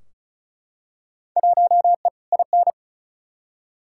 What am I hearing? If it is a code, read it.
Morse code, heard as a keyed tone, about 36 wpm, 700 Hz. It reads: 1E IN
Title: Field Day Exchanges - 35wpm